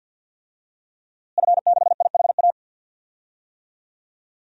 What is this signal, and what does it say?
Morse code, heard as a keyed tone, about 49 wpm, 700 Hz. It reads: F6IHA
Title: DXCC-100 Call Signs - 5 Characters - Repeated 3x - 50wpm